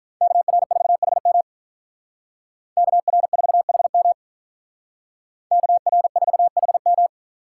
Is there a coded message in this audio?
Yes: Morse code from 0.2 to 7.1 s, heard as a keyed tone, about 46 wpm, 700 Hz. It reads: KR4HM KR4HM KR4HM